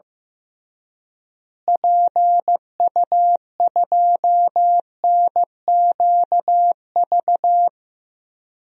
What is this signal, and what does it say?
Morse code, heard as a keyed tone, about 15 wpm, 700 Hz. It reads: PU2NQV
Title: DXCC-100 Call Signs - 6 Characters - Repeated 3x - 15wpm